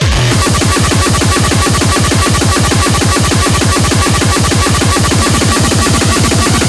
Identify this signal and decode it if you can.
There is no code.